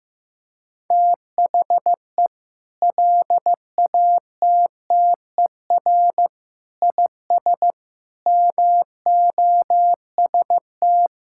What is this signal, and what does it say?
Morse code, heard as a keyed tone, about 16 wpm, 700 Hz. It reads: THE LATTER IS MOST
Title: Sentences from Top 1600 Words 15wpm